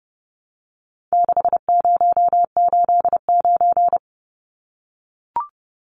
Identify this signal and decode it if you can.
Morse code, heard as a keyed tone, about 30 wpm, 700 Hz. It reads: 6089
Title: Numbers - 4-Digit 30wpm